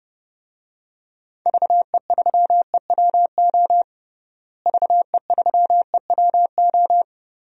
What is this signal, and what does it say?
Morse code, heard as a keyed tone, about 30 wpm, 700 Hz. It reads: VE3EWO VE3EWO
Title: DXCC-100 Call Signs - 6 Characters - Repeated 3x - 30wpm